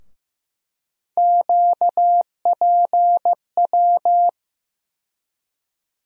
Morse code, heard as a keyed tone, about 15 wpm, 700 Hz. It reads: QPW
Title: USA Call Sign Postfixes - 3 Letters 15wpm